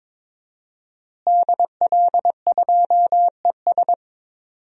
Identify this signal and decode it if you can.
Morse code, heard as a keyed tone, about 22 wpm, 700 Hz. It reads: DL2ES
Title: DXCC-100 Call Signs - 5 Characters - Repeated 2x - 22wpm